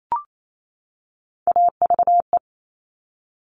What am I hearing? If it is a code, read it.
Morse code, heard as a keyed tone, about 28 wpm, 700 Hz. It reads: AVE
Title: Top Three Letter Combinations in Words 28wpm